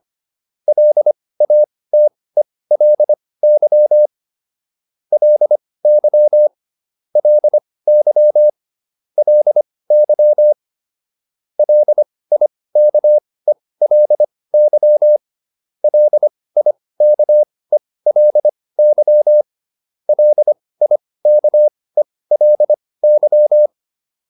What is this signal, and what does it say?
Morse code, heard as a keyed tone, about 25 wpm, spaced as for 18 wpm, 600 Hz. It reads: LATELY LY LY LY LIKELY LIKELY LIKELY